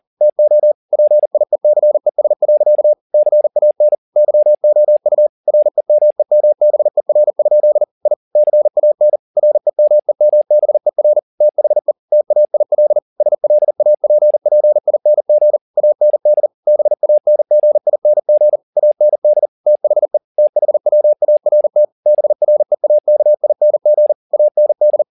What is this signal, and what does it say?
Morse code, heard as a keyed tone, about 40 wpm, 600 Hz. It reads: TO PIECES. CAN YOU REMEMBER? I CAN REMEMBER THE TAIL SLAPPING AND BANGING AND THE THWART BREAKING AND